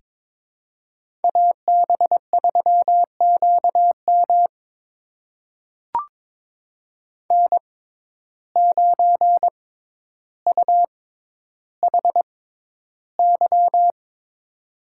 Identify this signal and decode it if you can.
Morse code, heard as a keyed tone, about 22 wpm, spaced as for 11 wpm, 700 Hz. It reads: AB3QM N9UHY
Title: USA Call Signs - 5 Characters - 6x Character Spacing - 22wpm